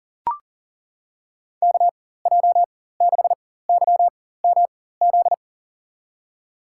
Morse code, heard as a keyed tone, about 38 wpm, spaced as for 18 wpm, 700 Hz. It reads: KJ6YMZ
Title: USA Call Signs - 6 Characters - 4x Character Spacing - 40wpm